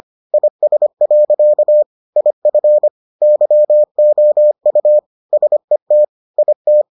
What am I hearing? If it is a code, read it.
Morse code, heard as a keyed tone, about 25 wpm, 600 Hz. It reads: IS. IF YOU SET IT